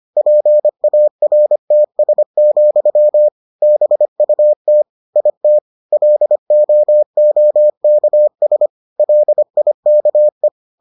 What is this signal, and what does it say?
Morse code, heard as a keyed tone, about 25 wpm, 600 Hz. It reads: PARTS, BUT IT LOOKS LIKE